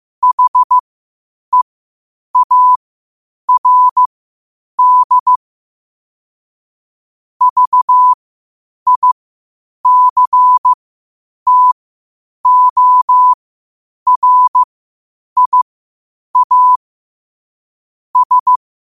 Morse code, heard as a keyed tone, about 15 wpm, spaced as for 8 wpm, 1000 Hz. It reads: HEARD VICTORIA S
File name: qotdfiles / 8wpm